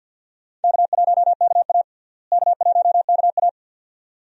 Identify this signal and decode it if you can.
Morse code, heard as a keyed tone, about 48 wpm, 700 Hz. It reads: K1KA K1KA
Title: USA Call Signs - 4 Characters - Repeated 2x - 50wpm